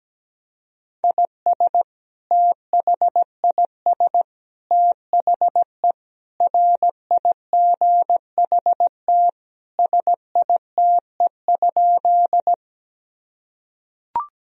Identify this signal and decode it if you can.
Morse code, heard as a keyed tone, about 17 wpm, 700 Hz. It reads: IS THIS THE RIGHT SITE?
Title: Sentences from Top 1000 Words - Review Part 4 - 17wpm